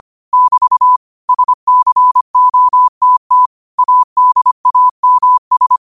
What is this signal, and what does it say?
Morse code, heard as a keyed tone, about 25 wpm, 1000 Hz. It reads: <BT> SCOTT ADAMS